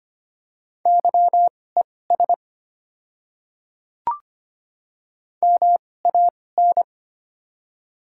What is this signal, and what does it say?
Morse code, heard as a keyed tone, about 25 wpm, spaced as for 18 wpm, 700 Hz. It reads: YES MAN
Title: 2x Character Spacing - Top 500 Words - 25wpm